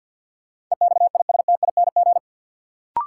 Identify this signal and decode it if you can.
Morse code, heard as a keyed tone, about 53 wpm, 700 Hz. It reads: EXISTING
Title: Sentences from Top 1500 Words - New Word Drill Alt - 50wpm